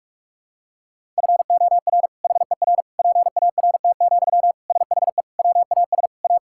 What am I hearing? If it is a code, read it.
Morse code, heard as a keyed tone, about 46 wpm, 700 Hz. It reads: FOR HER PART, SHE WAS A
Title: Sentences from Top 1000 Words - Review Part 5 - 45wpm